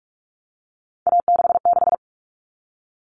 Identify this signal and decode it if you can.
Morse code, heard as a keyed tone, about 44 wpm, 700 Hz. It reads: A66
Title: All Cut Numbers - 3-Digit 45wpm